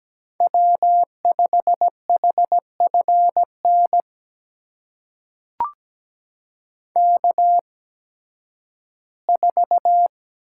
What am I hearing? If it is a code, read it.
Morse code, heard as a keyed tone, about 17 wpm, spaced as for 11 wpm, 700 Hz. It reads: W5HFN K4
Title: USA Call Signs - 8x Character Spacing - 17wpm